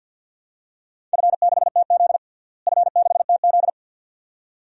Morse code, heard as a keyed tone, about 47 wpm, 700 Hz. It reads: F6TZ F6TZ
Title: DXCC-100 Call Signs - 4 Characters - Repeated 2x - 50wpm